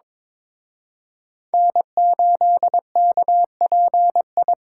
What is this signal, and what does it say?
Morse code, heard as a keyed tone, about 26 wpm, 700 Hz. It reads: N8KPI
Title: Speed Racing - USA 1x3 Call Signs 22wpm